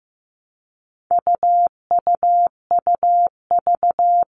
Morse code, heard as a keyed tone, about 15 wpm, 700 Hz. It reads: UUUV
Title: All Cut Numbers - 4-Digit 15wpm